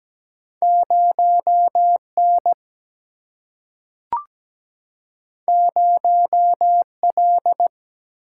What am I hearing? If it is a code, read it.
Morse code, heard as a keyed tone, about 17 wpm, 700 Hz. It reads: 0 N 0 L
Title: Single-Letter Single-Number 17wpm